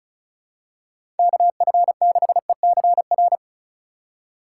Morse code, heard as a keyed tone, about 35 wpm, 700 Hz. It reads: KF6ECR
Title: USA Call Signs - 3x Character Spacing - 35wpm